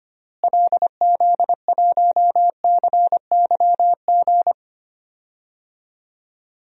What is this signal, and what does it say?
Morse code, heard as a keyed tone, about 25 wpm, 700 Hz. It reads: LZ1CYG
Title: DXCC-100 Call Signs - 6 Characters - Repeated 2x - 25wpm